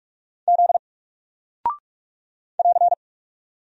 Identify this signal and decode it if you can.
Morse code, heard as a keyed tone, about 46 wpm, 700 Hz. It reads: Z <AR>
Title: Single Character (Letter:Number:Punctuation:Prosign) 45wpm